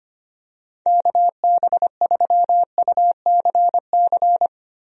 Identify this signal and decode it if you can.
Morse code, heard as a keyed tone, about 25 wpm, 700 Hz. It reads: KB3UCC
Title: USA Call Signs - 6 Characters - Repeated 3x - 25wpm